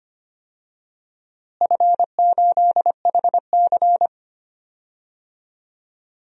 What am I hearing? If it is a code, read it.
Morse code, heard as a keyed tone, about 25 wpm, 700 Hz. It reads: F8HC
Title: DXCC-100 Call Signs - 4 Characters - Repeated 3x - 25wpm